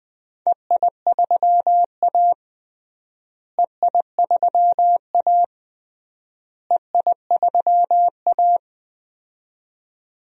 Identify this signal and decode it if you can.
Morse code, heard as a keyed tone, about 21 wpm, 700 Hz. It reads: EI3A EI3A EI3A
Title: DXCC-100 Call Signs - 4 Characters - Repeated 3x - 20wpm